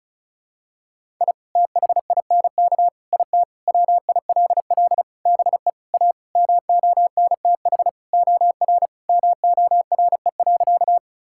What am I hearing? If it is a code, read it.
Morse code, heard as a keyed tone, about 34 wpm, 700 Hz. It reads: I THINK IT WILL BE A MONTH OR MORE.
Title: Sentences from Top 1000 Words - Review Part 5 - 35wpm